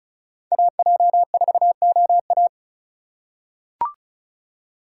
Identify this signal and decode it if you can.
Morse code, heard as a keyed tone, about 34 wpm, 700 Hz. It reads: AJ4OA
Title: USA Call Signs - 5 Characters - 4x Character Spacing - 35wpm